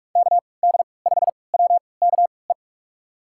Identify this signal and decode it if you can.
Morse code, heard as a keyed tone, about 47 wpm, spaced as for 26 wpm, 700 Hz. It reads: KD5WKE
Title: USA Call Signs - 3x Character Spacing - 45wpm